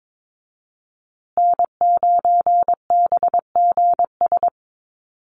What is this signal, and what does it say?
Morse code, heard as a keyed tone, about 22 wpm, 700 Hz. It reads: N9BGS
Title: USA Call Signs Repeated 3x 22wpm